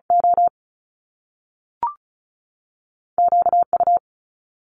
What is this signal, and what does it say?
Morse code, heard as a keyed tone, about 36 wpm, 700 Hz. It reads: O Q U
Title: Top Two Letter Combinations in Words 35wpm